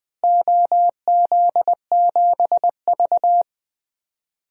Morse code, heard as a keyed tone, about 20 wpm, 700 Hz. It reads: OZ7V
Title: DXCC-100 Call Signs - 4 Characters - Repeated 3x - 20wpm